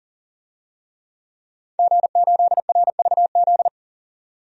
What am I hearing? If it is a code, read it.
Morse code, heard as a keyed tone, about 37 wpm, 700 Hz. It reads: G8RVZ
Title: DXCC-100 Call Signs - 5 Characters - 40wpm